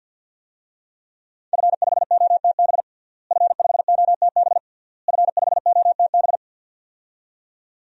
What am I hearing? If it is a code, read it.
Morse code, heard as a keyed tone, about 48 wpm, 700 Hz. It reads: F5OTB F5OTB F5OTB
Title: DXCC-100 Call Signs - 5 Characters - Repeated 3x - 50wpm